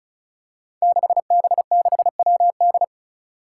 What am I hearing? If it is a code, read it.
Morse code, heard as a keyed tone, about 36 wpm, 700 Hz. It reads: 6B6WD
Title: Five Challenging Letter-Number Combinations 35wpm